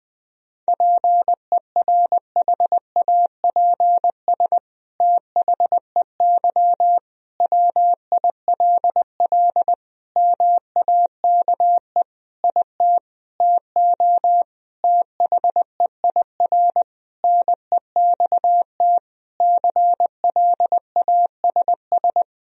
Morse code, heard as a keyed tone, about 20 wpm, 700 Hz. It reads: PERHAPS THEY WILL MAKE IT TO THEIR NEXT CLASS
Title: Sentences from Top 1000 Words - Review Part 2 - 20wpm